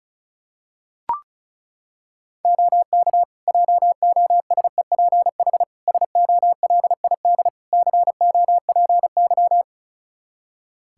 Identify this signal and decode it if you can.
Morse code, heard as a keyed tone, about 34 wpm, 700 Hz. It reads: OK JOSEPH SOLID COPY
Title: Practice QSO - Exchange 3 - More Info - 35wpm